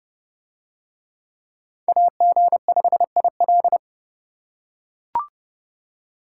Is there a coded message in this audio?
Yes: Morse code from 1.9 to 3.8 s, heard as a keyed tone, about 28 wpm, 700 Hz. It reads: AG5IL